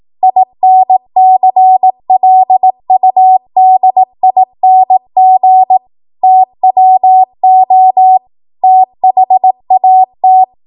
Morse code, heard as a keyed tone, about 18 wpm, 750 Hz. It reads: INCLUDING TWO THAT
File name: Morse / 18WPM